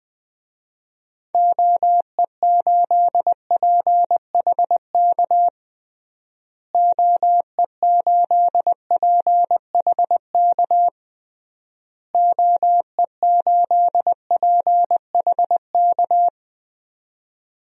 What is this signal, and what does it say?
Morse code, heard as a keyed tone, about 21 wpm, 700 Hz. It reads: OE8PHK OE8PHK OE8PHK
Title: DXCC-100 Call Signs - 6 Characters - Repeated 3x - 20wpm